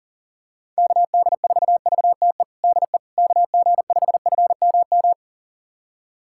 Morse code, heard as a keyed tone, about 40 wpm, 700 Hz. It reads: KD4VTE DE KG5FMM
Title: USA Callsign DE Callsign - 6 Characters - 40wpm